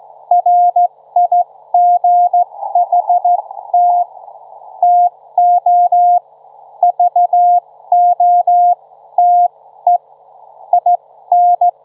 Morse code, heard as a keyed tone, about 15 wpm, spaced as for 13 wpm, 710 Hz. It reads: RIGHT TO VOTE IN
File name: Trivia-ZL2WT